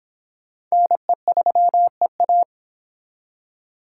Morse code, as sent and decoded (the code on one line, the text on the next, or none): -. ...-- . .-
NE3EA